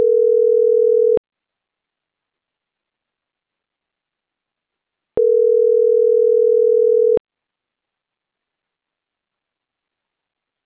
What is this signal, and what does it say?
Ringback tone (North America), 440 + 480 Hz.